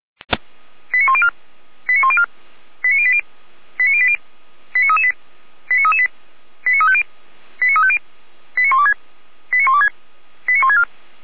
Five-tone selective call (ZVEI), heard as quick runs of five tones, tones of about 70 ms. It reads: 89104 89104 89080 89080 89338 89338 89350 89350 89126 89126 89164
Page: /455/SDR/ZVEI/